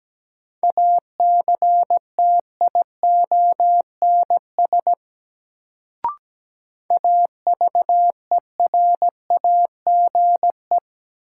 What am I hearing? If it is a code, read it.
Morse code, heard as a keyed tone, about 17 wpm, 700 Hz. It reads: ACTIONS AVERAGE
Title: Sentences from Top 1000 Words - Review Part 6 - New Word Drill - 17wpm